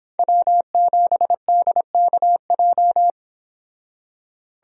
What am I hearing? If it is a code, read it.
Morse code, heard as a keyed tone, about 25 wpm, 700 Hz. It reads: W7DKJ